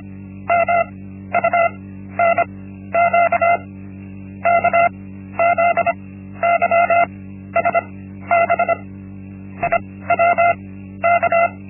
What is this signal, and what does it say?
Morse code, heard as a keyed tone, about 25 wpm, spaced as for 11 wpm, 680 Hz. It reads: MUNQK ZYSBI WK
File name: Linksmakalnis-1